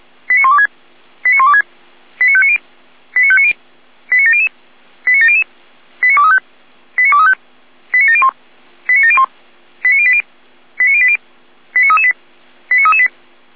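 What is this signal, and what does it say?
Five-tone selective call (ZVEI), heard as quick runs of five tones, tones of about 70 ms. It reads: 89126 89126 89590 89590 89700 89700 89235 89235 89701 89701 89080 89080 89338 89338
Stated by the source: ZVEI five-tone selective call received on an SDR.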